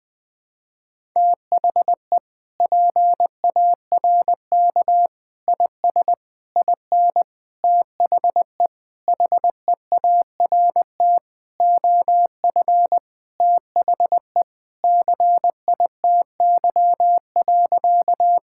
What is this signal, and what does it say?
Morse code, heard as a keyed tone, about 21 wpm, 700 Hz. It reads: THE PARK IS IN THE HEART OF THE CITY.
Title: Sentences with 4-Word Phrases - 20wpm